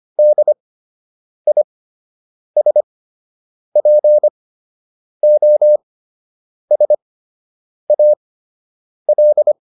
Morse code, heard as a keyed tone, about 25 wpm, spaced as for 8 wpm, 600 Hz. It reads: DISPOSAL